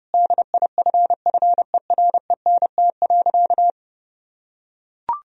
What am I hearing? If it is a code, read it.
Morse code, heard as a keyed tone, about 30 wpm, 700 Hz. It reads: DIFFERENT.
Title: Sentences from Top 1000 Words - Review Part 3 - 30wpm